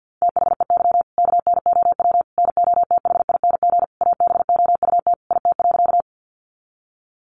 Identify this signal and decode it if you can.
Morse code, heard as a keyed tone, about 50 wpm, 700 Hz. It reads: THEY KNOW NOTHING ABOUT IT.
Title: Sentences from Top 500 Words 50wpm